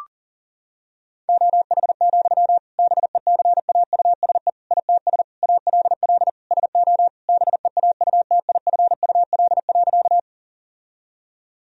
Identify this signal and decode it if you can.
Morse code, heard as a keyed tone, about 37 wpm, 700 Hz. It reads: OH, BECAUSE ITS ALL SO BEAUTIFUL.
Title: Sentences from Top 1000 Words - Review Part 3 - 40wpm